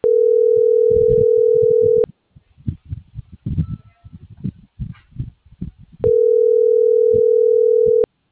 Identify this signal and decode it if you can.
Ringback tone (North America), 440 + 480 Hz.